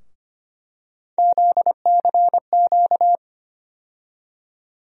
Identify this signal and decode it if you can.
Morse code, heard as a keyed tone, about 25 wpm, 700 Hz. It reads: ZCQ
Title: USA Call Sign Postfixes - 3 Letters 25wpm